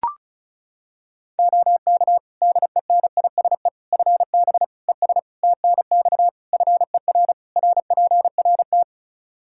Morse code, heard as a keyed tone, about 36 wpm, 700 Hz. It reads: OK DENISE FB ES TNX FER RPRT
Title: K1USN SST QSO Practice 35wpm